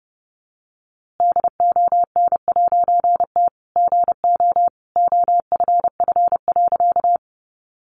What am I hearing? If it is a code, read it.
Morse code, heard as a keyed tone, about 30 wpm, 700 Hz. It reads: DON'T GO OFF.
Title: A Sound Of Thunder 30wpm